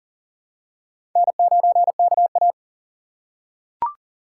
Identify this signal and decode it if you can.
Morse code, heard as a keyed tone, about 40 wpm, 700 Hz. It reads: N9KA
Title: USA Call Signs - 4 Characters - 6x Character Spacing - 40wpm